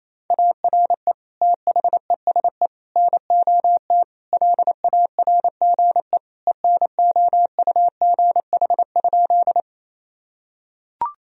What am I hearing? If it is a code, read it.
Morse code, heard as a keyed tone, about 28 wpm, 700 Hz. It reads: ARE THESE NOT LARGE ENOUGH?
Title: Sentences from Top 1000 Words - Review Part 1 - 28wpm